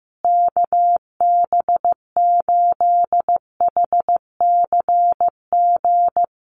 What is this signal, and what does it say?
Morse code, heard as a keyed tone, about 15 wpm, 700 Hz. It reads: KB8HCG
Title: USA Call Signs Repeated 2x 15wpm